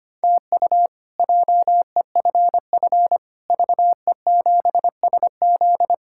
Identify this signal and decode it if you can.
Morse code, heard as a keyed tone, about 25 wpm, 700 Hz. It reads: TU JEFF VE7SZ
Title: K1USN SST QSO Practice - 25wpm